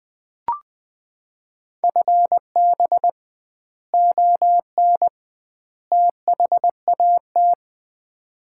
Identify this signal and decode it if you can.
Morse code, heard as a keyed tone, about 20 wpm, 700 Hz. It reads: FB ON THAT
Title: Common QSO Phrases - 2x Word-Spacing - 20wpm